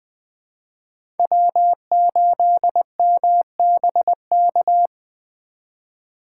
Morse code, heard as a keyed tone, about 20 wpm, 700 Hz. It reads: W8MBK